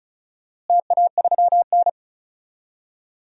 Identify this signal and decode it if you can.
Morse code, heard as a keyed tone, about 36 wpm, 690 Hz. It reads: TA3N
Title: DXCC-100 Call Signs - 4 Characters - Repeated 2x - 35wpm